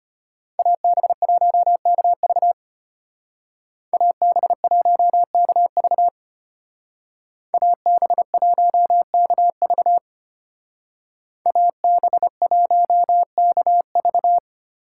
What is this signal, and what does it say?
Morse code, heard as a keyed tone, about 33 wpm, 700 Hz. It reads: AB1KV AB1KV AB1KV AB1KV